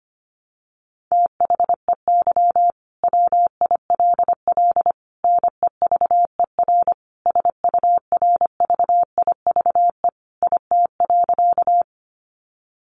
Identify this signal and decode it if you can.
Morse code, heard as a keyed tone, about 25 wpm, 700 Hz. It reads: THEY WILL NEVER SURVIVE IT.
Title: Sentences from Top 1300 Words 25wpm